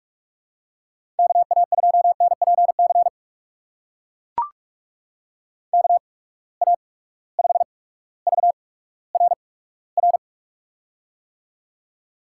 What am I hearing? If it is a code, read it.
Morse code, heard as a keyed tone, about 41 wpm, spaced as for 19 wpm, 700 Hz. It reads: KA2NPC KA5VRR
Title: USA Call Signs - 6 Characters - 8x Character Spacing - 45wpm